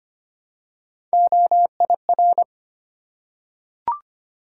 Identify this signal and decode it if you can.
Morse code, heard as a keyed tone, about 25 wpm, 700 Hz. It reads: OIR
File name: Three Letter-Number Combinations - 25wpm